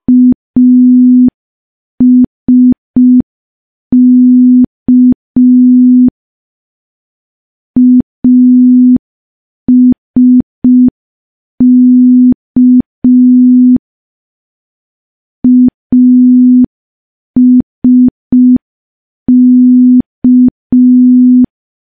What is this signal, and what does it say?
Morse code, heard as a keyed tone, about 5 wpm, 250 Hz. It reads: ASK ASK ASK
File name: morse / morse-training-05WPM